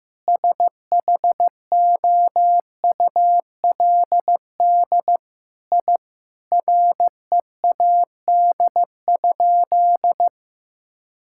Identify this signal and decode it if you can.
Morse code, heard as a keyed tone, about 15 wpm, 700 Hz. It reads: SHOULD I READ?
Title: Sentences from Top 1000 Words - Review Part 5 - 15wpm